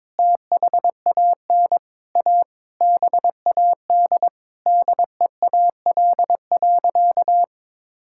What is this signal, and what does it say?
Morse code, heard as a keyed tone, about 22 wpm, 700 Hz. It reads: THAN A BAD DEAL.
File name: practice / 22wpm / Sentences with 8-Word Phrases - 22wpm